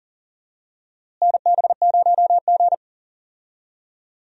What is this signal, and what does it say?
Morse code, heard as a keyed tone, about 37 wpm, 700 Hz. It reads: NB0G
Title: USA Call Signs - 4 Characters - Repeated 3x - 40wpm